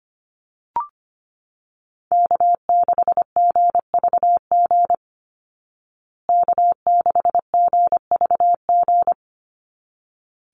Morse code, heard as a keyed tone, about 25 wpm, 700 Hz. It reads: K6GVG K6GVG
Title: Call Signs Repeated 3x - 25wpm